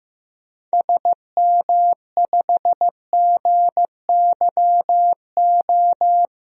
Morse code, heard as a keyed tone, about 15 wpm, 700 Hz. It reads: SM5GYO
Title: DXCC-100 Call Signs - 6 Characters - 15wpm